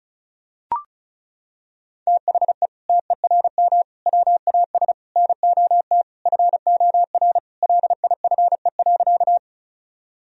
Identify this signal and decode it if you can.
Morse code, heard as a keyed tone, about 36 wpm, 700 Hz. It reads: THE TERM WAS NOT FOR LIFE.
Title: Sentences from Top 1000 Words - Review Part 4 - 35wpm